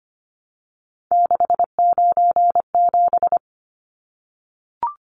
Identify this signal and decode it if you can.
Morse code, heard as a keyed tone, about 25 wpm, 700 Hz. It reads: 697
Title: Numbers - 3-Digit 25wpm